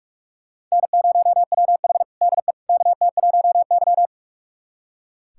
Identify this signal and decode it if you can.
Morse code, heard as a keyed tone, about 42 wpm, 690 Hz. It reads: N0WH DE KT1Y
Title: USA Callsign DE Callsign - 4 Characters - 45wpm